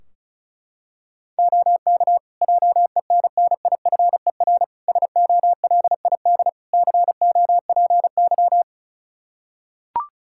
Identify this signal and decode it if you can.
Morse code, heard as a keyed tone, about 36 wpm, 700 Hz. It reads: OK JENNIFER SOLID COPY
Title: Practice QSO - Exchange 3 - More Info - 35wpm